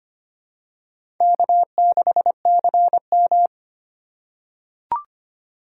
Morse code, heard as a keyed tone, about 25 wpm, 700 Hz. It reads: K6CM
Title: USA Call Signs - 4 Characters - 3x Character Spacing - 25wpm